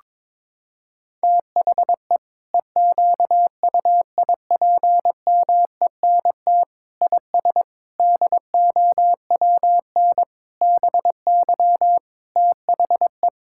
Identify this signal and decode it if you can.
Morse code, heard as a keyed tone, about 22 wpm, 700 Hz. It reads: THE EQUIPMENT IS DOWN BY THE
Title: Sentences from Top 1000 Words - Review Part 5 - 22wpm